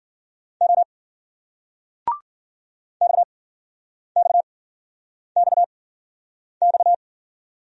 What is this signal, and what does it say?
Morse code, heard as a keyed tone, about 55 wpm, spaced as for 9 wpm, 700 Hz. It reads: O XXXX